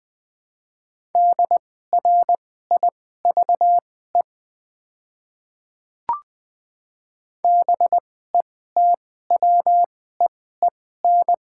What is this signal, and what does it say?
Morse code, heard as a keyed tone, about 20 wpm, spaced as for 15 wpm, 690 Hz. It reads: DRIVE BETWEEN
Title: 2x Character Spacing - Top 500 Words - 20wpm